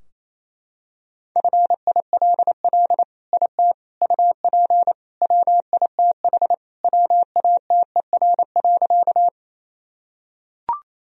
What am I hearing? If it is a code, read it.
Morse code, heard as a keyed tone, about 27 wpm, 700 Hz. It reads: FILL IT UP WITH WATER.
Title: Sentences from Top 1000 Words - Review Part 3 - 28wpm